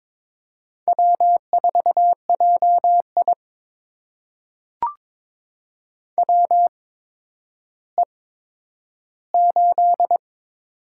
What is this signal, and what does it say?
Morse code, heard as a keyed tone, about 23 wpm, spaced as for 11 wpm, 700 Hz. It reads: W4JI WE8
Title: USA Call Signs - 4 Characters - 8x Character Spacing - 22wpm